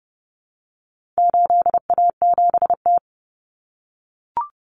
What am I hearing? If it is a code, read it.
Morse code, heard as a keyed tone, about 30 wpm, 700 Hz. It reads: 8A7T
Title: Common Cut Numbers - 4-Digit 30wpm